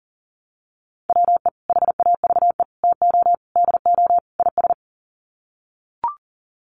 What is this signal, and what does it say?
Morse code, heard as a keyed tone, about 40 wpm, 700 Hz. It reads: WE HAVE TO DO IS
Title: Sets of 5 Words Encore 40wpm